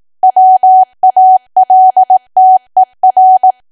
Morse code, heard as a keyed tone, about 18 wpm, 750 Hz. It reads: WALTER
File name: Archive / 18 WPM / 18WPM